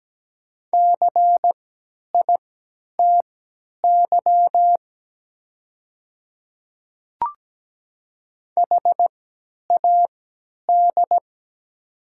Morse code, heard as a keyed tone, about 17 wpm, spaced as for 10 wpm, 700 Hz. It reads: CITY HAD
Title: 3x Character Spacing - Top 300 Words - 17wpm